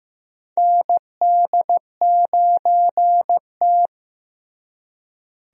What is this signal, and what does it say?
Morse code, heard as a keyed tone, about 15 wpm, 700 Hz. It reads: ND9T
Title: USA Call Signs - 4 Characters - Repeated 2x - 15wpm